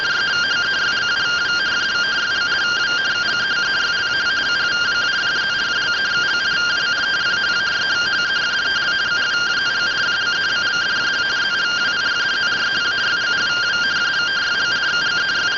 Two-tone teleprinter signal, about 170 Hz shift.